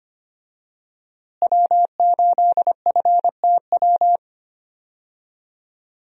Morse code, heard as a keyed tone, about 25 wpm, 700 Hz. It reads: W8FTW
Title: USA Call Signs - 5 Characters - Repeated 2x - 25wpm